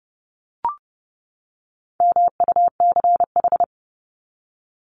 Morse code, heard as a keyed tone, about 30 wpm, 700 Hz. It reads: MUCH
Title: Top 200 Words 30wpm